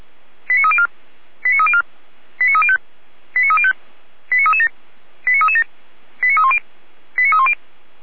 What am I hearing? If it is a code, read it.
Five-tone selective call (ZVEI), heard as quick runs of five tones, tones of about 70 ms. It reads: 89304 89304 89305 89305 89337 89337 89310 89310